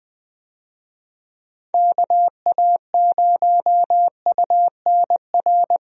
Morse code, heard as a keyed tone, about 20 wpm, 700 Hz. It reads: KA0UNR
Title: USA Call Signs - 6 Characters - Repeated 2x - 20wpm